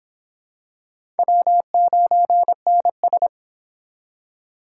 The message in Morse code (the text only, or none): W9NS